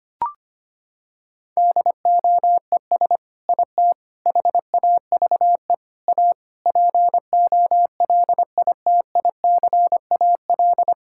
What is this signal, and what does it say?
Morse code, heard as a keyed tone, about 25 wpm, 700 Hz. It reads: DOES IT HAVE A POLITICAL
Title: Sentences from Top 1000 Words - Review Part 5 - 25wpm